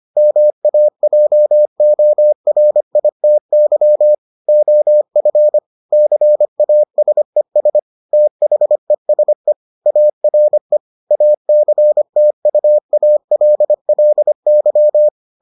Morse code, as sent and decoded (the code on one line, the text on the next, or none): -- .- .--- --- .-. .. - -.-- / --- ..-. / -.-. .- ... ... / - .... ... / .- .-. / .- -.-. - ..- .- .-.. .-.. -.--
MAJORITY OF CASES THESE ARE ACTUALLY